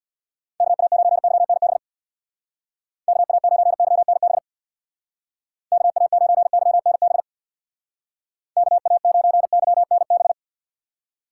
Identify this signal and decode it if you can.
Morse code, heard as a keyed tone, about 60 wpm, 700 Hz. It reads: KA9YNB KA9YNB KA9YNB KA9YNB
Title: Speed Racing - USA 2x3 Call Signs 50wpm